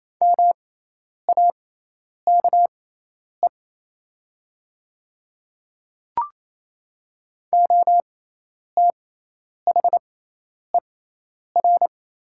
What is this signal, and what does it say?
Morse code, heard as a keyed tone, about 28 wpm, spaced as for 10 wpm, 700 Hz. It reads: MAKE OTHER
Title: 6x Character Spacing - Top 100 Words - 28wpm